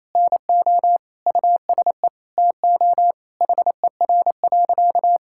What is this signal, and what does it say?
Morse code, heard as a keyed tone, about 27 wpm, 700 Hz. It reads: NO USE TO HER.
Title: Sentences from Top 1000 Words - Review Part 6 - 28wpm